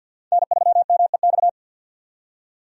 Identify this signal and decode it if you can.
Morse code, heard as a keyed tone, about 52 wpm, 700 Hz. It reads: N3MEX
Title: USA Call Signs - 5 Characters - 50wpm